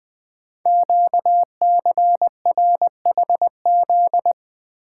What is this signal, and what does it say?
Morse code, heard as a keyed tone, about 20 wpm, 700 Hz. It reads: QCRHZ
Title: Code Groups - 5 Letters - 20wpm